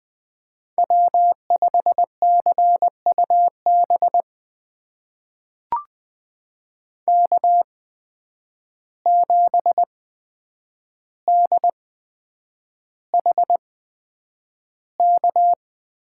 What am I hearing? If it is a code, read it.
Morse code, heard as a keyed tone, about 20 wpm, spaced as for 9 wpm, 700 Hz. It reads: W5CUB K7DHK
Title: USA Call Signs - 5 Characters - 8x Character Spacing - 20wpm